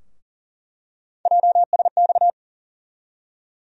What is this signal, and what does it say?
Morse code, heard as a keyed tone, about 40 wpm, 700 Hz. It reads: JSX
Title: USA Call Sign Postfixes - 3 Letters 40wpm